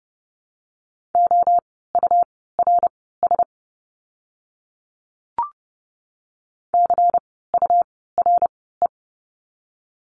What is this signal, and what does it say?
Morse code, heard as a keyed tone, about 30 wpm, spaced as for 17 wpm, 700 Hz. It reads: OURS CURE
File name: Lesson 17f - Letter u - Words 30wpm